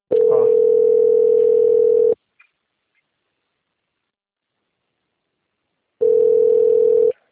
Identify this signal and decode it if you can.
Ringback tone (North America), 440 + 480 Hz.
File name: out- - - - - .17935